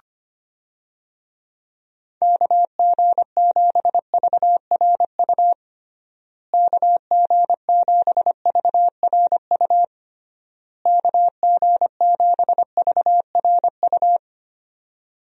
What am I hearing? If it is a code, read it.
Morse code, heard as a keyed tone, about 25 wpm, 700 Hz. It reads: KG7VRU KG7VRU KG7VRU
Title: USA Call Signs - 6 Characters - Repeated 3x - 25wpm